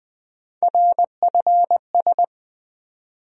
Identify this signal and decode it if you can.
Morse code, heard as a keyed tone, about 20 wpm, 700 Hz. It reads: RFS
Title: USA Call Sign Postfixes - 3 Letters 20wpm